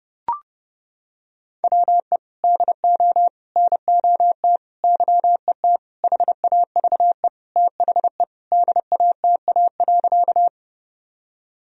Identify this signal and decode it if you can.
Morse code, heard as a keyed tone, about 30 wpm, 700 Hz. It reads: WE DO NOT YET HAVE THE DATA.
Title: Sentences from Top 1000 Words - Review Part 3 - 30wpm